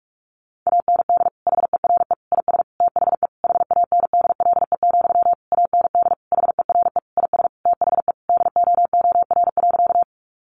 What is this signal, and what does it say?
Morse code, heard as a keyed tone, about 44 wpm, 700 Hz. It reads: AND HERE IS THE HANDLE, AND HERE IS THE DOOR.
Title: Sentences from Top 1100 Words 45wpm